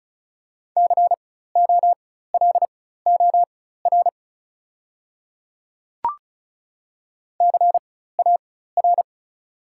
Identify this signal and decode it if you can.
Morse code, heard as a keyed tone, about 36 wpm, spaced as for 16 wpm, 690 Hz. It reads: COLOR CAR
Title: 4x Character Spacing - Top 500 Words - 35wpm